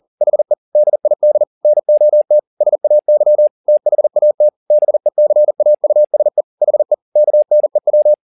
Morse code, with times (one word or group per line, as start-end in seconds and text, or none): 0.2-0.5 HE
0.8-1.4 DID
1.6-2.4 NOT
2.6-3.5 SAY
3.7-4.5 THAT
4.7-6.4 BECAUSE
6.6-6.9 HE
7.2-8.1 KNEW